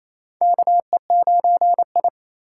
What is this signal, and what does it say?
Morse code, heard as a keyed tone, about 28 wpm, 700 Hz. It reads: KE9I
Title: K1USN SST QSO Practice - 28wpm